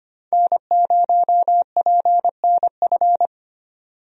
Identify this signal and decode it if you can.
Morse code, heard as a keyed tone, about 25 wpm, 700 Hz. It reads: N0PNF
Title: USA Call Signs - 5 Characters - 25wpm